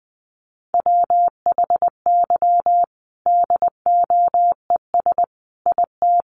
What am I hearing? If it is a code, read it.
Morse code, heard as a keyed tone, about 20 wpm, 700 Hz. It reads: WHY DOES IT
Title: Sentences from Top 700 Words 20wpm